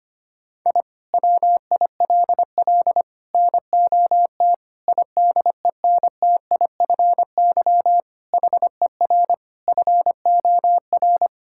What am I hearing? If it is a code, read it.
Morse code, heard as a keyed tone, about 25 wpm, 700 Hz. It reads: I WILL NOT IDENTIFY HER FOR
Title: Sentences from Top 1000 Words - Review Part 6 - 25wpm